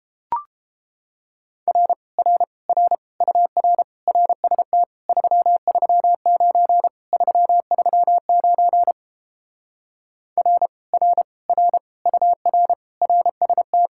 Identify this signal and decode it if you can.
Morse code, heard as a keyed tone, about 29 wpm, 700 Hz. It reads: R R R UR RST 339 339 R R R UR RST